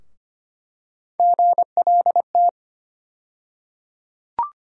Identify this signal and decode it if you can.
Morse code, heard as a keyed tone, about 25 wpm, 700 Hz. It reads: GLT
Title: USA Call Sign Postfixes - 3 Letters 25wpm